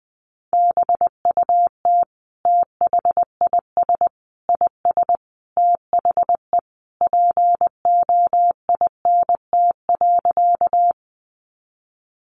Morse code, heard as a keyed tone, about 19 wpm, 700 Hz. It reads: BUT THIS IS THE POINT.
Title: Sentences from Top 200 Words 20wpm